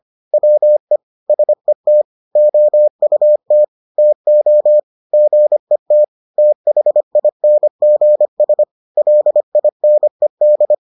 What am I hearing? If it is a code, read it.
Morse code, heard as a keyed tone, about 25 wpm, 600 Hz. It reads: WE SET OUT TO GET THINGS LINED